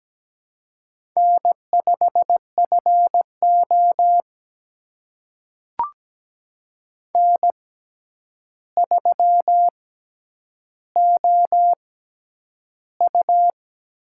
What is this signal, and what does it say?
Morse code, heard as a keyed tone, about 17 wpm, spaced as for 9 wpm, 700 Hz. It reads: N5FO N3OU
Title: USA Call Signs - 4 Characters - 6x Character Spacing - 17wpm